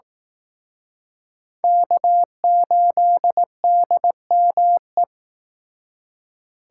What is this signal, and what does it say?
Morse code, heard as a keyed tone, about 18 wpm, 700 Hz. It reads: K8DME
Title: Speed Racing - USA Call Signs 15wpm